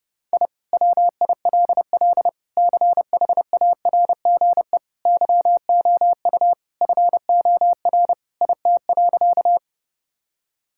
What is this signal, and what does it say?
Morse code, heard as a keyed tone, about 30 wpm, 700 Hz. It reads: I WILL CHARGE YOU FOR IT.
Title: Sentences from Top 1000 Words - Review Part 6 - 30wpm